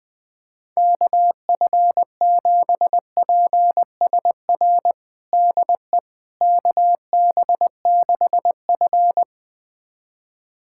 Morse code, heard as a keyed tone, about 21 wpm, 700 Hz. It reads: KF7PSR DE KB6F
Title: USA Callsign DE Callsign 20wpm